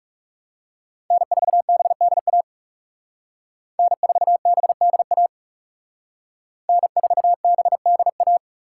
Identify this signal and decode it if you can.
Morse code, heard as a keyed tone, about 38 wpm, 700 Hz. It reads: N4BDA N4BDA N4BDA